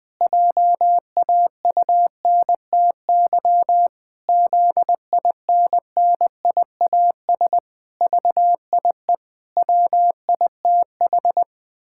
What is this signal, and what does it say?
Morse code, heard as a keyed tone, about 19 wpm, 700 Hz. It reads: JAUNTY ZINNIAS VIE WITH
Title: Pangram Challenge v2 - 20wpm